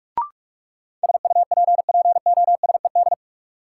Morse code, heard as a keyed tone, about 42 wpm, 700 Hz. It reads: SUPPOSED
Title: Sets of 3 Words - New Word Drill - 45wpm